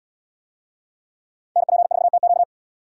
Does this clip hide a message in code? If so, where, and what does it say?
Yes: Morse code from 1.6 to 2.4 s, heard as a keyed tone, about 69 wpm, 700 Hz. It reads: NP4TQ